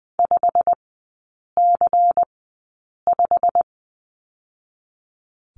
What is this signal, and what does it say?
Morse code, heard as a keyed tone, about 20 wpm, spaced as for 8 wpm, 700 Hz. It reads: 5C5
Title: 198 Lesson 32e - Letter k - Three Characters 20wpm